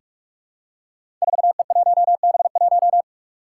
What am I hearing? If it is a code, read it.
Morse code, heard as a keyed tone, about 46 wpm, 700 Hz. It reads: 4E1B1
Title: Five Challenging Letter-Number Combinations 45wpm